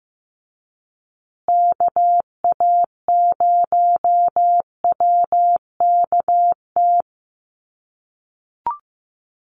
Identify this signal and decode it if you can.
Morse code, heard as a keyed tone, about 15 wpm, 700 Hz. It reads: KA0WKT
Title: USA Call Signs Repeated 2x 15wpm